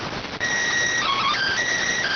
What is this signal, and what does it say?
SSTV (slow-scan television), PD 290.